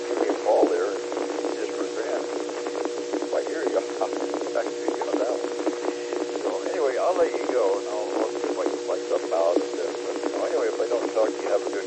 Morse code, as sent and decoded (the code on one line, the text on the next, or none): none